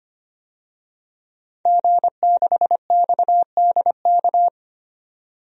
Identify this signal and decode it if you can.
Morse code, heard as a keyed tone, about 25 wpm, 700 Hz. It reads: G6XDK
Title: DXCC-100 Call Signs - 5 Characters - 25wpm